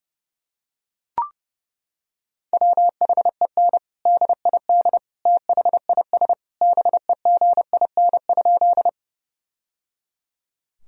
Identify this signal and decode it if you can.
Morse code, heard as a keyed tone, about 30 wpm, 700 Hz. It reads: WHEN DID THIS BEGIN?
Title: Sentences from Top 1000 Words - Review Part 1 - 30wpm